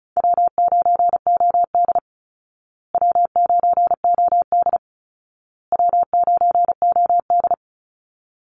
Morse code, heard as a keyed tone, about 34 wpm, 700 Hz. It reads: W9OD W9OD W9OD
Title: Easy Call Signs Repeated 3x - 35wpm